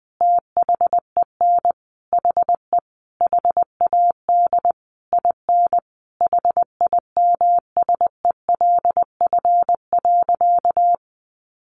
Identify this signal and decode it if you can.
Morse code, heard as a keyed tone, about 20 wpm, 700 Hz. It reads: THEN HE HAD IN HIMSELF.
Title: Sentences from Top 1500 Words 20wpm